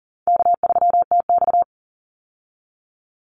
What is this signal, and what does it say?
Morse code, heard as a keyed tone, about 40 wpm, 700 Hz. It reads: K3TX
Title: Call Signs - 40wpm